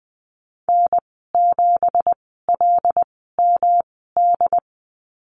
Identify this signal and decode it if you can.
Morse code, heard as a keyed tone, about 20 wpm, spaced as for 15 wpm, 700 Hz. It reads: N7LMD
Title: 167 Lesson 27g - Number 7 - Call Signs 20wpm